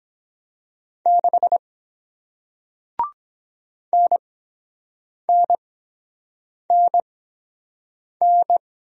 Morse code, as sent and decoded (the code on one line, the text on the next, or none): -.... / -. -. -. -.
6 NNNN